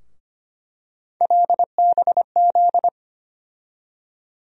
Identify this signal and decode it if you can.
Morse code, heard as a keyed tone, about 25 wpm, 700 Hz. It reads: LBZ
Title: USA Call Sign Postfixes - 3 Letters 25wpm